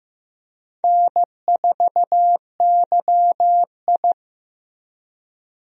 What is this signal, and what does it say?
Morse code, heard as a keyed tone, about 15 wpm, 700 Hz. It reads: N4YI
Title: USA Call Signs - 4 Characters - Repeated 2x - 15wpm